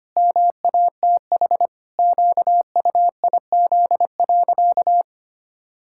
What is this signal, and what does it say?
Morse code, heard as a keyed tone, about 25 wpm, 700 Hz. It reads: MATH QUIZ.
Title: Pangram Challenge v2 - 25wpm